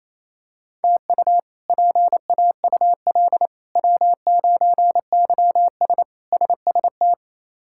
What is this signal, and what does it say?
Morse code, heard as a keyed tone, about 28 wpm, 700 Hz. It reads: TU PAUL W9YS SST
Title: K1USN SST QSO Practice - 28wpm